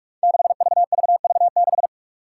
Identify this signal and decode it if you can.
Morse code, heard as a keyed tone, about 46 wpm, 700 Hz. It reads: 6VVV6
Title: Five Challenging Letter-Number Combinations 45wpm